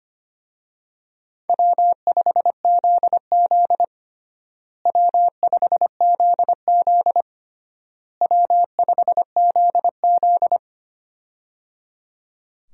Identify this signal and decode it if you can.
Morse code, heard as a keyed tone, about 25 wpm, 700 Hz. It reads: W5ZZ W5ZZ W5ZZ
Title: USA Call Signs - 4 Characters - Repeated 3x - 25wpm